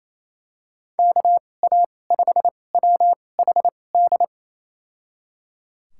Morse code, heard as a keyed tone, about 28 wpm, spaced as for 20 wpm, 700 Hz. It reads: KA5WHD
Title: USA Call Signs - 6 Characters - 2x Character Spacing - 28wpm